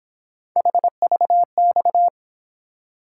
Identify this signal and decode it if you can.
Morse code, heard as a keyed tone, about 26 wpm, 700 Hz. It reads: HVX
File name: Speed-Racing - Three Letter-Number Combinations - 22wpm